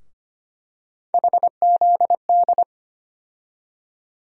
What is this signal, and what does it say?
Morse code, heard as a keyed tone, about 25 wpm, 700 Hz. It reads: HZD
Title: USA Call Sign Postfixes - 3 Letters 25wpm